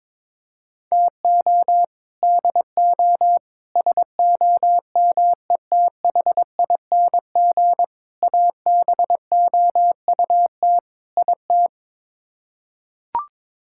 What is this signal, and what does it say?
Morse code, heard as a keyed tone, about 22 wpm, 700 Hz. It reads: TO DO SOMETHING ABOUT IT
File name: practice / 22wpm / Build A Phrase - 5-Word Phrases - 22wpm